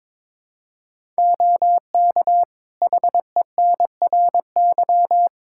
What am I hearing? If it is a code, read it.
Morse code, heard as a keyed tone, about 22 wpm, 700 Hz. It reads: OK HENRY
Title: Practice QSO - Exchange 3 - More Info - 22wpm